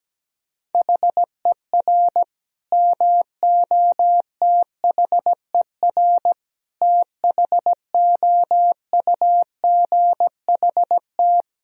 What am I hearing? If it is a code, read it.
Morse code, heard as a keyed tone, about 17 wpm, 700 Hz. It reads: HER MOTHER THOUGHT
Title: Sentences from Top 1000 Words - Review Part 4 - 17wpm